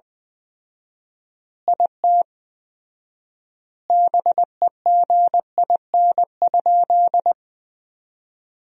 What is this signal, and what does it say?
Morse code, heard as a keyed tone, about 20 wpm, 700 Hz. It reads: IT BEGIN?
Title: 4x Word-Spacing - Sentences from Top 300 Words - 20wpm